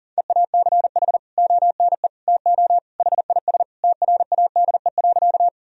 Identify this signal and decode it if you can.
Morse code, heard as a keyed tone, about 37 wpm, 700 Hz. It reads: EACH ONE TO HIS TRADE.
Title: Sentences from Top 1000 Words - Review Part 4 - 40wpm